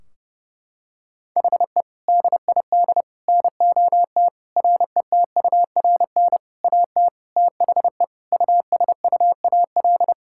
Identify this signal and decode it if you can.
Morse code, heard as a keyed tone, about 30 wpm, 700 Hz. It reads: HE DID NOT RETURN AT THE USUAL
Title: Sentences from Top 1600 Words 30wpm